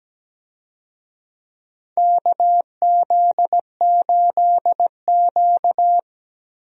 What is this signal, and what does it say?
Morse code, heard as a keyed tone, about 17 wpm, 700 Hz. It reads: KZ8Q
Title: USA Call Signs - 4 Characters - Repeated 2x - 17wpm